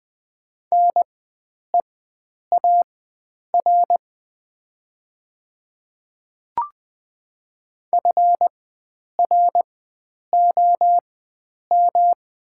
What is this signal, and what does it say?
Morse code, heard as a keyed tone, about 20 wpm, spaced as for 9 wpm, 700 Hz. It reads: NEAR FROM
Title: 4x Character Spacing - Top 500 Words - 20wpm